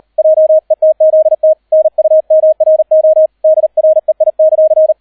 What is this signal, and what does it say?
Morse code, heard as a keyed tone, about 38 wpm, 630 Hz, with radio noise. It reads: JETZT NUMRO DREI;